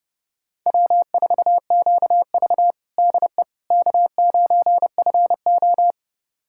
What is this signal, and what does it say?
Morse code, heard as a keyed tone, about 30 wpm, 700 Hz. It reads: W4QV DE K9FO
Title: USA Callsign DE Callsign - 4 Characters - 30wpm